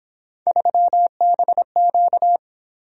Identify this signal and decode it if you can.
Morse code, heard as a keyed tone, about 27 wpm, 700 Hz. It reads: 3BQ
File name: Speed-Racing - Three Letter-Number Combinations - 22wpm